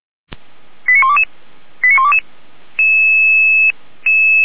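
Five-tone selective call (ZVEI), heard as quick runs of five tones, tones of about 70 ms. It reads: 89122 89122
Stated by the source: ZVEI five-tone call 89131 received on an SDR.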